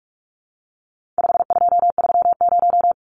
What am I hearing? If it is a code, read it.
Morse code, heard as a keyed tone, about 44 wpm, 700 Hz. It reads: 5230
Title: Numbers - 4-Digit 45wpm